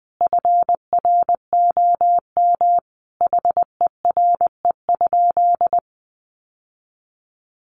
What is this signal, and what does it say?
Morse code, heard as a keyed tone, about 20 wpm, 700 Hz. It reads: FROM HERE?
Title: Sentences from Top 1100 Words 20wpm